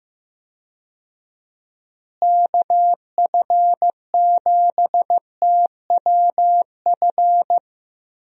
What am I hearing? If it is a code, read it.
Morse code, heard as a keyed tone, about 16 wpm, 700 Hz. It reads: KF7TWF
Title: USA Call Signs - 6 Characters - Repeated 3x - 15wpm